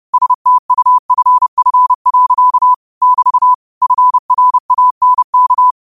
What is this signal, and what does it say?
Morse code, heard as a keyed tone, about 30 wpm, 1000 Hz. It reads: STUFF. <BT> FRANK